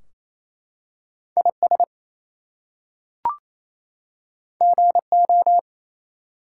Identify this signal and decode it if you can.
Morse code, heard as a keyed tone, about 27 wpm, 700 Hz. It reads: IS GO
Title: Two Letter Words - 28wpm